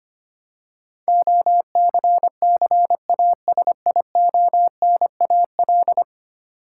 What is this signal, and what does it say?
Morse code, heard as a keyed tone, about 25 wpm, 700 Hz. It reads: OCCASIONAL
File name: 10 Letter Words - First 3 Letters Given - 25wpm